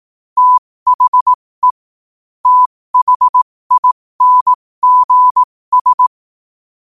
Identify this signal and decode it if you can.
Morse code, heard as a keyed tone, about 18 wpm, spaced as for 15 wpm, 1000 Hz. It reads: THE THINGS
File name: qotdfiles / 15wpm